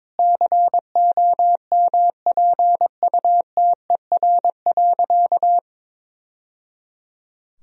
Morse code, heard as a keyed tone, about 22 wpm, 700 Hz. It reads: COMPUTER.
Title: Sentences from Top 1000 Words - Review Part 3 - 22wpm